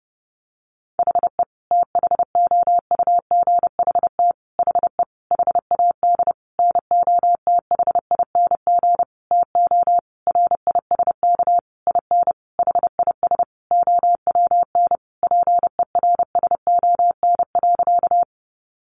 Morse code, heard as a keyed tone, about 30 wpm, 700 Hz. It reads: HE THOUGHT HE HAD NOTHING TO RISK IN HIS OWN PERSON.
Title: Sentences from Top 600 Words 30wpm